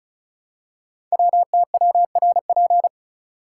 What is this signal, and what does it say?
Morse code, heard as a keyed tone, about 36 wpm, 700 Hz. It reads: WTWRP
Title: Code Groups - 5 Letters - 35wpm